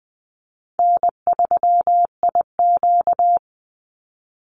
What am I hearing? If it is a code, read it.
Morse code, heard as a keyed tone, about 20 wpm, 700 Hz. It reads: N3IQ
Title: Easy Call Signs - 20wpm